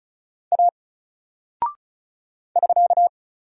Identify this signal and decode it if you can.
Morse code, heard as a keyed tone, about 35 wpm, 700 Hz. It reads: A <SK>
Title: Single Character (Letter:Number:Punctuation:Prosign) Alt 35wpm